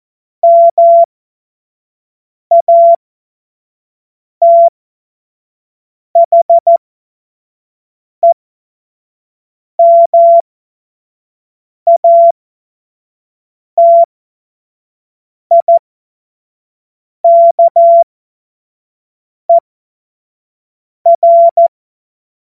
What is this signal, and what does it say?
Morse code, heard as a keyed tone, about 14 wpm, spaced as for 5 wpm, 680 Hz. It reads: MATHEMATIKER
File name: Morsekurs / Witze / Witz 21. Dezember 2021, 50 F70